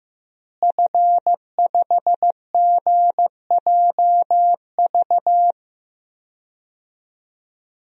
Morse code, heard as a keyed tone, about 14 wpm, 700 Hz. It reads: F5GJV
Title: DXCC-100 Call Signs - 5 Characters - Repeated 3x - 15wpm